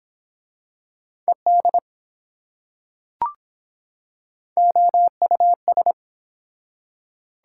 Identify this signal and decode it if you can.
Morse code, heard as a keyed tone, about 23 wpm, 700 Hz. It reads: ED OUS